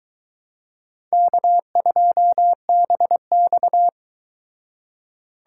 Morse code, heard as a keyed tone, about 22 wpm, 700 Hz. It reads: K2BX